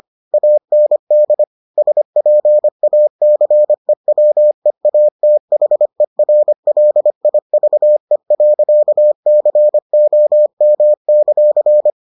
Morse code, heard as a keyed tone, about 25 wpm, 600 Hz. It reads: AND SPACEWEATHERLIVE.COM;